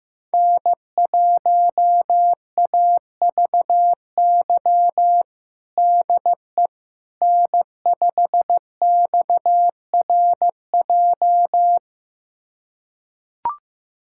Morse code, heard as a keyed tone, about 15 wpm, 700 Hz. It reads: N1AVY DE N5XRJ
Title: USA Callsign DE Callsign - 5 Characters - 15wpm